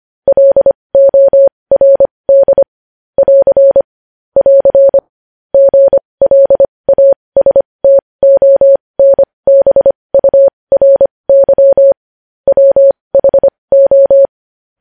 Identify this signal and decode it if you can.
Morse code, heard as a keyed tone, about 25 wpm, spaced as for 20 wpm, 550 Hz. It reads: LORD <AR> <AR> GLASTONBURY WHO
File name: podcast / News headlines-1x-25x20